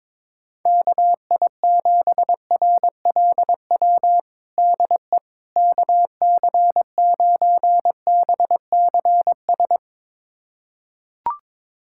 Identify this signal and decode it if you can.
Morse code, heard as a keyed tone, about 22 wpm, 700 Hz. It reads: KI7RLW DE KC9BCS
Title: USA Callsign DE Callsign - 6 Characters - 22wpm